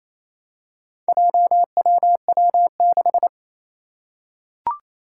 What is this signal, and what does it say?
Morse code, heard as a keyed tone, about 29 wpm, 700 Hz. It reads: JWW6
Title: Four Challenging Letter-Number Combinations 28wpm